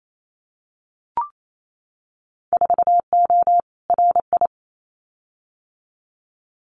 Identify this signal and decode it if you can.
Morse code, heard as a keyed tone, about 28 wpm, 700 Hz. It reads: 4O RI
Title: Winter Field Day Exchanges Encore 28wpm